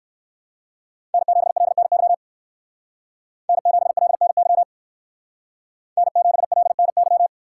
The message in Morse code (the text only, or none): N7LNY N7LNY N7LNY